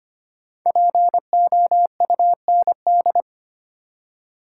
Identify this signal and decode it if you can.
Morse code, heard as a keyed tone, about 25 wpm, 700 Hz. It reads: POUND
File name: 5 Letter Words - First 3 Letters Given - 25wpm